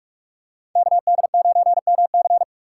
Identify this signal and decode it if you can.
Morse code, heard as a keyed tone, about 44 wpm, 700 Hz. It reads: KD9MC
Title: USA Call Signs - 5 Characters - 3x Character Spacing - 45wpm